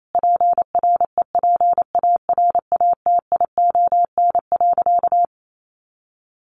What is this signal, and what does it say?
Morse code, heard as a keyed tone, about 28 wpm, 700 Hz. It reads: PREPARATION.
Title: Sentences from Top 1400 Words 28wpm